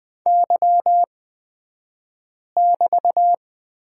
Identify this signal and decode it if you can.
Morse code, heard as a keyed tone, about 20 wpm, spaced as for 5 wpm, 700 Hz. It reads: Y<BT>
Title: 223 Lesson 36d - BT - Two Characters 20wpm